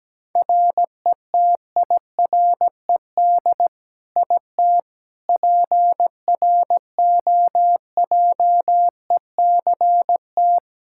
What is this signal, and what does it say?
Morse code, heard as a keyed tone, about 17 wpm, 700 Hz. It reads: RETIRED IT PROJECT
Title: Practice QSO - Exchange 3 - More Info - 17wpm